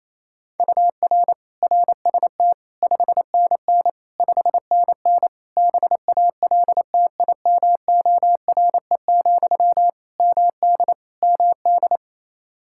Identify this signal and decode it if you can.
Morse code, heard as a keyed tone, about 28 wpm, 700 Hz. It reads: UR RST 5NN 5NN BALTIMORE, MD MD
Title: Practice QSO - Exchange 1 - Beginning - 28wpm